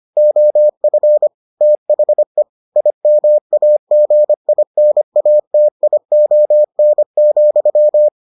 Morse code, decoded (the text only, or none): OF THE IMAGINATION,